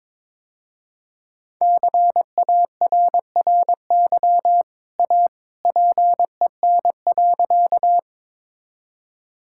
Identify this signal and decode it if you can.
Morse code, heard as a keyed tone, about 22 wpm, 700 Hz. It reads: CARRY A PEN.
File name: practice / 22wpm / Easy Sentences with Hints - 22wpm